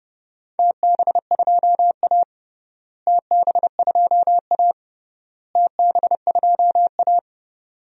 Morse code, heard as a keyed tone, about 30 wpm, 700 Hz. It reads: TB2A TB2A TB2A
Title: DXCC-100 Call Signs - 4 Characters - Repeated 3x - 30wpm